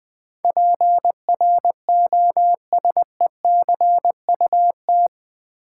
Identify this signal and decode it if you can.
Morse code, heard as a keyed tone, about 20 wpm, 700 Hz. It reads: PROSECUT
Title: Build A Word - 4 Letters At A Time - 11-Letter Words - 20wpm